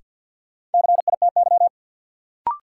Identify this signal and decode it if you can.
Morse code, heard as a keyed tone, about 53 wpm, 700 Hz. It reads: CITY
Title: Sets of 2 Words - New Word Drill - 50wpm